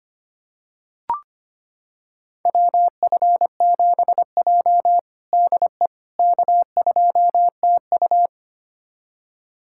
Morse code, heard as a keyed tone, about 25 wpm, 700 Hz. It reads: WF7J DE K2TU
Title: USA Callsign DE Callsign - 4 Characters - 25wpm